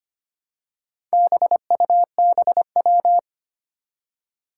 Morse code, heard as a keyed tone, about 25 wpm, 700 Hz. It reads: BUBW
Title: Four Challenging Letter-Number Combinations 25wpm